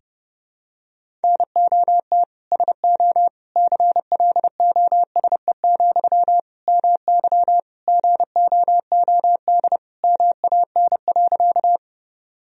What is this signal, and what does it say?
Morse code, heard as a keyed tone, about 30 wpm, 700 Hz. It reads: NOT SO CLOSE, MY GOOD MAN.
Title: Sentences from Top 1000 Words - Review Part 1 - 30wpm